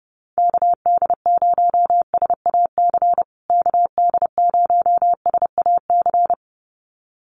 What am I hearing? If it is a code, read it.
Morse code, heard as a keyed tone, about 30 wpm, 700 Hz. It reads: KD0SAC KD0SAC
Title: USA Call Signs Repeated 2x 30wpm